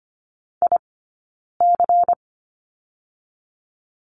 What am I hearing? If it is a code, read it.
Morse code, heard as a keyed tone, about 24 wpm, 700 Hz. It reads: I C
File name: course / 25:10wpm / 092 Lesson 16d - Letter c - Two Characters 25wpm